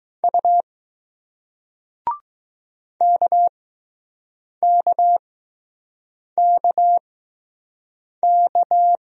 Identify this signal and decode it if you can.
Morse code, heard as a keyed tone, about 20 wpm, spaced as for 6 wpm, 700 Hz. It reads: U KKKK